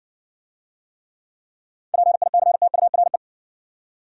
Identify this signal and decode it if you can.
Morse code, heard as a keyed tone, about 58 wpm, 700 Hz. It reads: PICTURE